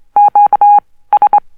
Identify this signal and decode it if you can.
Morse code, heard as a keyed tone, about 23 wpm, spaced as for 16 wpm, 820 Hz. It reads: QS
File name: morse code oscillator - low pitched NzH